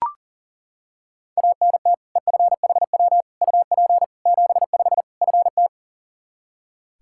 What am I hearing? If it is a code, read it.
Morse code, heard as a keyed tone, about 40 wpm, 700 Hz. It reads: ANT EFHW UP 75 FT
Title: K1USN SST QSO Practice 40wpm